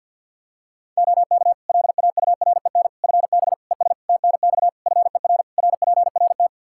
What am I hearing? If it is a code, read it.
Morse code, heard as a keyed tone, about 51 wpm, 700 Hz. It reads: OK LAUREN FB ES TNX FER RPRT
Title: Practice QSO - Exchange 2 - Info - 50wpm